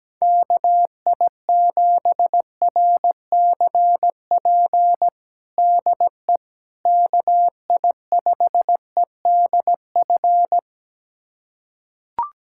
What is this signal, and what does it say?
Morse code, heard as a keyed tone, about 17 wpm, 700 Hz. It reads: KI7RCP DE KI5EDF
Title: USA Callsign DE Callsign 17wpm